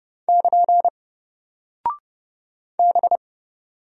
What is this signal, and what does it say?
Morse code, heard as a keyed tone, about 30 wpm, 700 Hz. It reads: <KN> B
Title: Single Character (Letter:Number:Punctuation:Prosign) Alt 30wpm